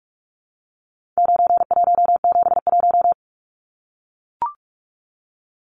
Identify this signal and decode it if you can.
Morse code, heard as a keyed tone, about 44 wpm, 700 Hz. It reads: 9171
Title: Numbers - 4-Digit 45wpm